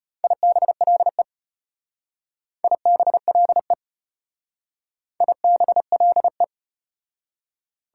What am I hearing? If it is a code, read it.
Morse code, heard as a keyed tone, about 34 wpm, 700 Hz. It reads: IBLE IBLE IBLE